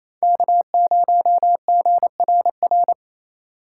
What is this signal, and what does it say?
Morse code, heard as a keyed tone, about 36 wpm, 700 Hz. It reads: K0GRR